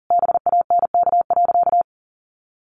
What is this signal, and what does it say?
Morse code, heard as a keyed tone, about 40 wpm, 700 Hz. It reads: BANK.
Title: Sentences from Top 1500 Words 40wpm